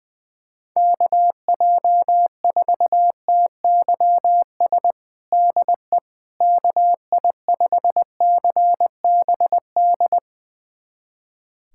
Morse code, heard as a keyed tone, about 20 wpm, 700 Hz. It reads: KJ4TYS DE KI5CBD
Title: USA Callsign DE Callsign - 6 Characters - 20wpm